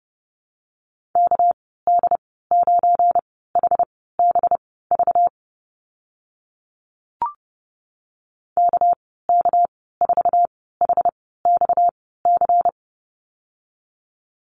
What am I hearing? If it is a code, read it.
Morse code, heard as a keyed tone, about 30 wpm, spaced as for 17 wpm, 700 Hz. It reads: KD9HBV KK4HXC
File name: Lesson 37g - Letter x - Call Signs 30wpm